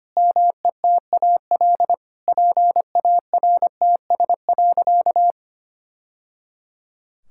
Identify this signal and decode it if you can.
Morse code, heard as a keyed tone, about 25 wpm, 700 Hz. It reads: METAL PARTS.
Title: Sentences from Top 1000 Words - Review Part 6 - 25wpm